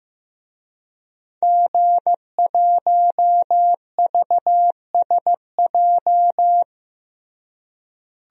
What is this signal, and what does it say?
Morse code, heard as a keyed tone, about 15 wpm, 700 Hz. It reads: G1VSJ
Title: DXCC-100 Call Signs - 5 Characters - Repeated 2x - 15wpm